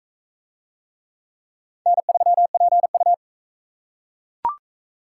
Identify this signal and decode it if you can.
Morse code, heard as a keyed tone, about 40 wpm, 700 Hz. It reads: N3PU